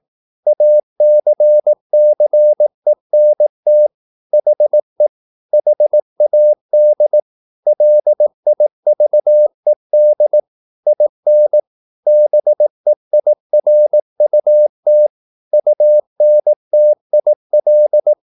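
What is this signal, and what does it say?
Morse code, heard as a keyed tone, about 18 wpm, 600 Hz. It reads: ACCENT HE HAD LIVED IN BEIRUT UNTIL
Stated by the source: Morse practice at 18 wpm, English text, from CW med Gnister.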